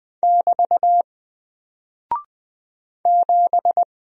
Morse code, heard as a keyed tone, about 20 wpm, 700 Hz. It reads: <BT> 7
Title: Single Character (Letter:Number:Punctuation:Prosign) 20wpm